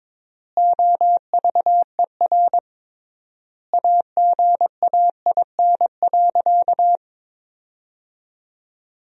Morse code, heard as a keyed tone, about 22 wpm, 700 Hz. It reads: OVER AGAIN.
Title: 3x Word-Spacing - Sentences from Top 500 Words - 22wpm